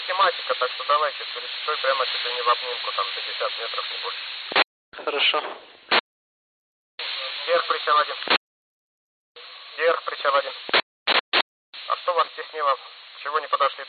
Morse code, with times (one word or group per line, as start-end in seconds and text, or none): none